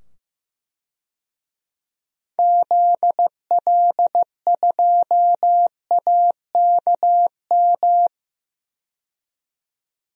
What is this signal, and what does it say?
Morse code, heard as a keyed tone, about 15 wpm, 700 Hz. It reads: ZL2AKM
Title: DXCC-100 Call Signs - 6 Characters - 15wpm